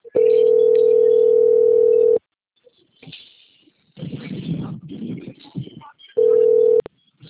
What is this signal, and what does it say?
Ringback tone (North America), 440 + 480 Hz.